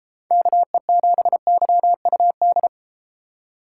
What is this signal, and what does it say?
Morse code, heard as a keyed tone, about 31 wpm, 700 Hz. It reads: KE7YUD